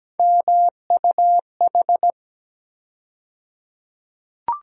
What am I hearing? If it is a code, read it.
Morse code, heard as a keyed tone, about 17 wpm, 700 Hz. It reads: MUH
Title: USA Call Sign Postfixes - 3 Letters 17wpm